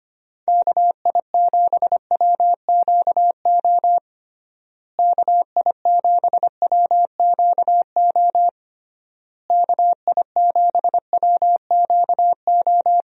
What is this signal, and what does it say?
Morse code, heard as a keyed tone, about 25 wpm, 700 Hz. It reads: KI7WQO KI7WQO KI7WQO
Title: USA Call Signs - 6 Characters - Repeated 3x - 25wpm